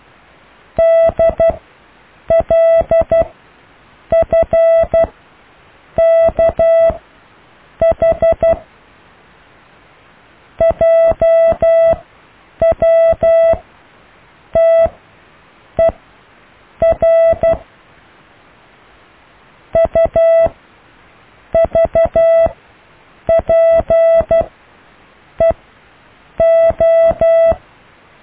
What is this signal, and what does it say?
Morse code, heard as a keyed tone, about 12 wpm, spaced as for 7 wpm, 660 Hz. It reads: DLFKH JWTER UVPEO
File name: cw / IJLUM+CDRO+EKTPV+BFHNW